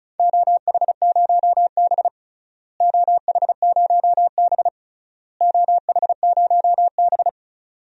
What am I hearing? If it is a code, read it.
Morse code, heard as a keyed tone, about 36 wpm, 700 Hz. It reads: OH0B OH0B OH0B
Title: DXCC-100 Call Signs - 4 Characters - Repeated 3x - 35wpm